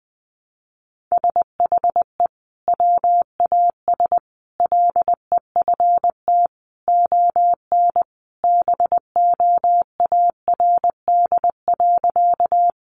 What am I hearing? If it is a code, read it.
Morse code, heard as a keyed tone, about 20 wpm, 700 Hz. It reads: SHE WAS LEFT ON BOARD.
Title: Sentences from Top 700 Words 20wpm